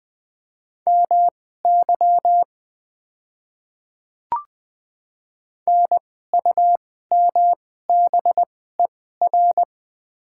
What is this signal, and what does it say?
Morse code, heard as a keyed tone, about 20 wpm, spaced as for 14 wpm, 700 Hz. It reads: MY NUMBER
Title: 2x Character Spacing - Top 100 Words - 20wpm